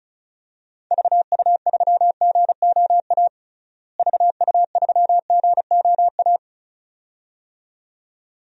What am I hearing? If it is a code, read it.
Morse code, heard as a keyed tone, about 36 wpm, 700 Hz. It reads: VU3GOA VU3GOA
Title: DXCC-100 Call Signs - 6 Characters - Repeated 2x - 35wpm